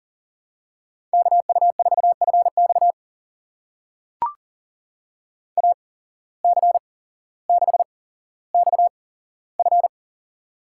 Morse code, heard as a keyed tone, about 39 wpm, spaced as for 17 wpm, 700 Hz. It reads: KU4FX AC6XF
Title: USA Call Signs - 5 Characters - 8x Character Spacing - 40wpm